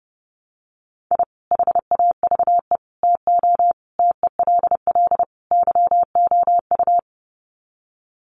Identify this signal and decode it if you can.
Morse code, heard as a keyed tone, about 30 wpm, 700 Hz. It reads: I HAVE TO TELL YOU
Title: Sets of 5 Words 30wpm